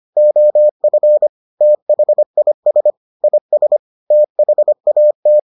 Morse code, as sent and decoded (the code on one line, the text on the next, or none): --- ..-. / - .... .. ... / .. ... / - .... .- -
OF THIS IS THAT